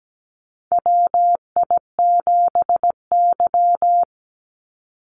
Morse code, heard as a keyed tone, about 17 wpm, 700 Hz. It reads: WI7Y
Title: USA Call Signs 17wpm